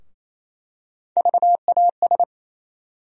Morse code, heard as a keyed tone, about 28 wpm, 700 Hz. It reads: VAS
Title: USA Call Sign Postfixes - 3 Letters 28wpm